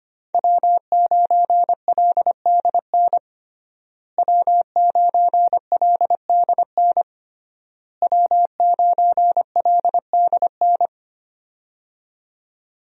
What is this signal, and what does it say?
Morse code, heard as a keyed tone, about 25 wpm, 700 Hz. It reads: W9LDN W9LDN W9LDN
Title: USA Call Signs - 5 Characters - Repeated 3x - 25wpm